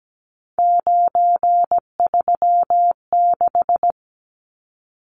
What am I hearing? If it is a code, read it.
Morse code, heard as a keyed tone, about 17 wpm, 700 Hz. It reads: 936
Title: Numbers - 3-Digit 17wpm